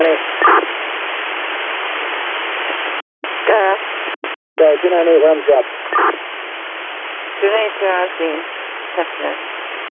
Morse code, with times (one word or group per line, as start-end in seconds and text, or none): none